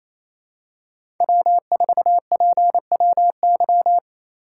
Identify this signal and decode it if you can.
Morse code, heard as a keyed tone, about 28 wpm, 700 Hz. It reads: W4PWY
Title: USA Call Signs - 4x Character Spacing - 28wpm